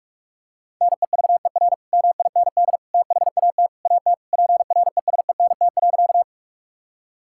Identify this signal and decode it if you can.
Morse code, heard as a keyed tone, about 46 wpm, 700 Hz. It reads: NEVER MIND THAT AT PRESENT.
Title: Sentences from Top 1000 Words - Review Part 5 - 45wpm